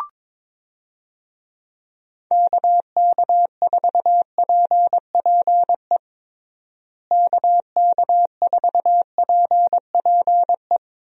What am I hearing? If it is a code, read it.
Morse code, heard as a keyed tone, about 22 wpm, 700 Hz. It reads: KK4PPE KK4PPE
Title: USA Call Signs - 6 Characters - Repeated 3x - 22wpm